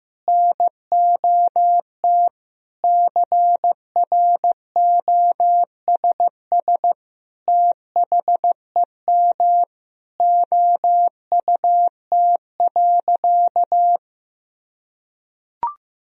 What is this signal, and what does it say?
Morse code, heard as a keyed tone, about 16 wpm, 700 Hz. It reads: NOT CROSS THEM OUT.
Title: Sentences from Top 1000 Words - Review Part 1 - 15wpm